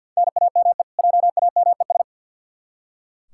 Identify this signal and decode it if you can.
Morse code, heard as a keyed tone, about 46 wpm, 700 Hz. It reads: NAME JAMES
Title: Practice QSO - Exchange 1 - Beginning - 50wpm